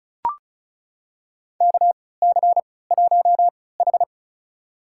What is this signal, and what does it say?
Morse code, heard as a keyed tone, about 36 wpm, spaced as for 20 wpm, 700 Hz. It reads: KC1H
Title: USA Call Signs - 3x Character Spacing - 35wpm